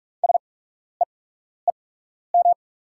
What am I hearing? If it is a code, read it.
Morse code, heard as a keyed tone, about 44 wpm, spaced as for 12 wpm, 690 Hz. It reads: SEEM
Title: 8x Character Spacing - Top 300 Words - 45wpm